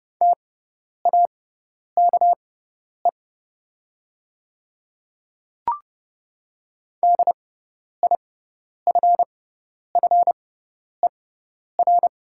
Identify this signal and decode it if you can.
Morse code, heard as a keyed tone, about 30 wpm, spaced as for 10 wpm, 690 Hz. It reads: TAKE DIFFER
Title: 6x Character Spacing - Top 200 Words - 30wpm